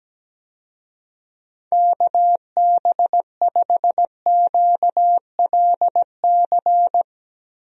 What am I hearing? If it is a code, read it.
Morse code, heard as a keyed tone, about 17 wpm, 700 Hz. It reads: KB5QLC
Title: USA Call Signs - 6 Characters - Repeated 2x - 17wpm